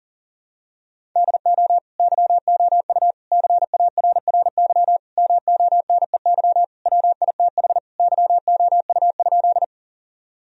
Morse code, heard as a keyed tone, about 40 wpm, 700 Hz. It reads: DO YOU CARRY MONEY WITH YOU?
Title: Sentences from Top 1000 Words - Review Part 1 - 40wpm